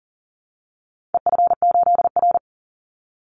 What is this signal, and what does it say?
Morse code, heard as a keyed tone, about 40 wpm, 700 Hz. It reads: EF8R
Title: Call Signs Repeated 2x - 40wpm